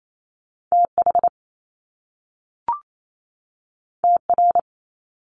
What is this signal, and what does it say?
Morse code, heard as a keyed tone, about 28 wpm, 700 Hz. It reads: TH TR
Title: Top Two Letter Combinations in Words 28wpm